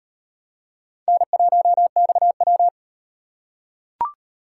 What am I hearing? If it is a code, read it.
Morse code, heard as a keyed tone, about 37 wpm, 700 Hz. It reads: N1XW